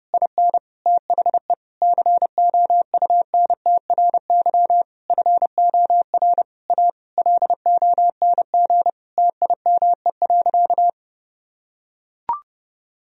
Morse code, heard as a keyed tone, about 30 wpm, 700 Hz. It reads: IN THE COUNTRY FOR A LONG TIME.
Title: Sentences from Top 1000 Words - Review Part 1 - 30wpm